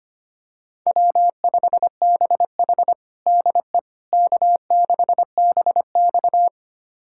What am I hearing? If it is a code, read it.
Morse code, heard as a keyed tone, about 25 wpm, 700 Hz. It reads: W5BH DE K6BX
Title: USA Callsign DE Callsign - 4 Characters - 25wpm